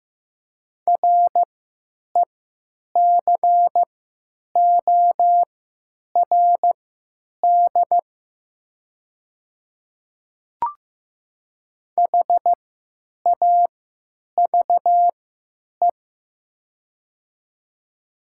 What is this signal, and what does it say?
Morse code, heard as a keyed tone, about 15 wpm, spaced as for 9 wpm, 700 Hz. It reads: RECORD HAVE
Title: 3x Character Spacing - Top 500 Words - 15wpm